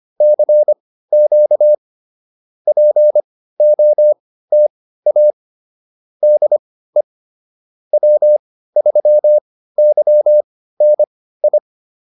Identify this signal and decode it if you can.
Morse code, heard as a keyed tone, about 25 wpm, spaced as for 15 wpm, 600 Hz. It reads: CQ POTA DE W3YNI
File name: POTA101 15